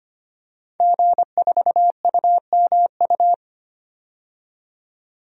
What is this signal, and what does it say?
Morse code, heard as a keyed tone, about 25 wpm, 700 Hz. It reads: G4UMU
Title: DXCC-100 Call Signs - 5 Characters - 25wpm